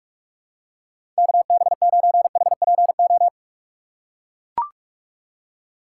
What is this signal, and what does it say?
Morse code, heard as a keyed tone, about 44 wpm, 700 Hz. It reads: KB9HPO
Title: USA Call Signs - 6x Character Spacing - 45wpm